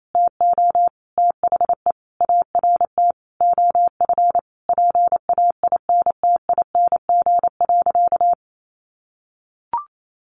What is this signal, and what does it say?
Morse code, heard as a keyed tone, about 28 wpm, 700 Hz. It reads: TO THE ART OF PAINTING.
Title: Sentences from Top 1200 Words 28wpm